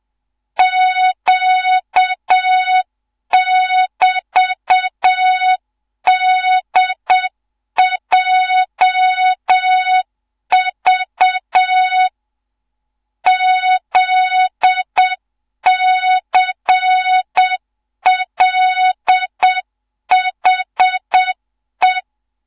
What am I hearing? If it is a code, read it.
Morse code, heard as a keyed tone, about 7 wpm, 3700 Hz, with noise. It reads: Q<BT>DJV ZCLHE
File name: Morze-HA2MN / hang / 15 GYAKORLAT